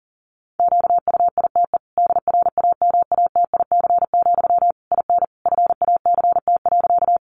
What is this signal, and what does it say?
Morse code, heard as a keyed tone, about 40 wpm, 700 Hz. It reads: QUITE DRAMATIC, IN FACT.
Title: Sentences from Top 1300 Words 40wpm